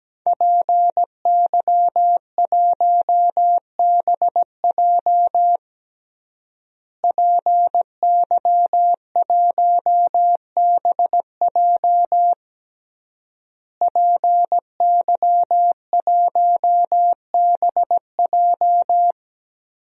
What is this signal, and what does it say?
Morse code, heard as a keyed tone, about 17 wpm, 700 Hz. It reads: PY1BJ PY1BJ PY1BJ
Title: DXCC-100 Call Signs - 5 Characters - Repeated 3x - 17wpm